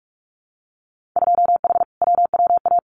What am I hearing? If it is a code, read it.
Morse code, heard as a keyed tone, about 46 wpm, 700 Hz. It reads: 2H WWA
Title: Winter Field Day Exchanges 45wpm